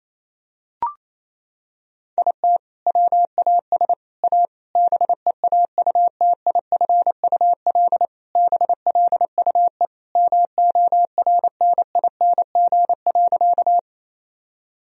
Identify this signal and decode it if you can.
Morse code, heard as a keyed tone, about 28 wpm, 700 Hz. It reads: IT WAS A BEAUTIFUL BLUE MORNING.
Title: Sentences from Top 1000 Words - Review Part 3 - 28wpm